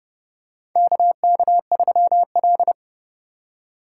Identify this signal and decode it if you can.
Morse code, heard as a keyed tone, about 30 wpm, 700 Hz. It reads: KK3L
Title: USA Call Signs - 4 Characters - 8x Character Spacing - 30wpm